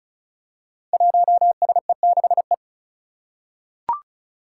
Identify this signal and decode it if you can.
Morse code, heard as a keyed tone, about 36 wpm, 700 Hz. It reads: 1SE6E
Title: Five Challenging Letter-Number Combinations 35wpm